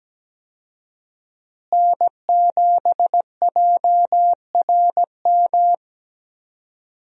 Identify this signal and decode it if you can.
Morse code, heard as a keyed tone, about 17 wpm, 700 Hz. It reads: N7JRM
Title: USA Call Signs - 5 Characters - Repeated 2x - 17wpm